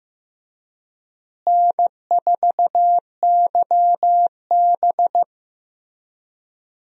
Morse code, heard as a keyed tone, about 15 wpm, 700 Hz. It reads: N4YB
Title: USA Call Signs - 4 Characters - Repeated 3x - 15wpm